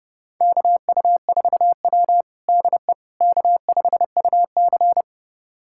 Morse code, heard as a keyed tone, about 30 wpm, 700 Hz. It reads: KU4W DE K5UC
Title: USA Callsign DE Callsign - 4 Characters - 30wpm